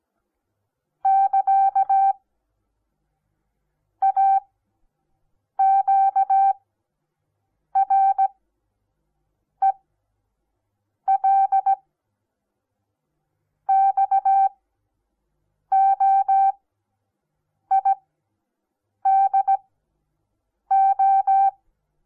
Morse code, heard as a keyed tone, about 17 wpm, spaced as for 6 wpm, 780 Hz. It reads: <KA> AQREL XOIDO
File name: Onlinelehrgang / morsen / DJ4UF Morsen08